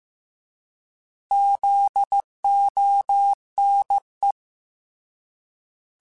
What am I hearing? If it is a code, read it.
Morse code, heard as a keyed tone, about 15 wpm, 800 Hz. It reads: ZONE